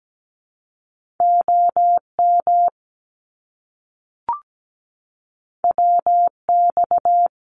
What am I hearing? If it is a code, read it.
Morse code, heard as a keyed tone, about 17 wpm, 700 Hz. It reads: OM WX
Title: Abbreviations Repeated 3x - 17wpm